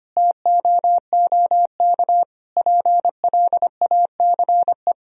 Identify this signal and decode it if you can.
Morse code, heard as a keyed tone, about 25 wpm, 700 Hz. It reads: TOOK PLACE
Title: Sentences from Top 1000 Words - Review Part 4 - 25wpm